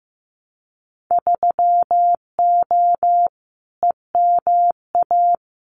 Morse code, heard as a keyed tone, about 15 wpm, 700 Hz. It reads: 3O EMA
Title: Winter Field Day Exchanges 15wpm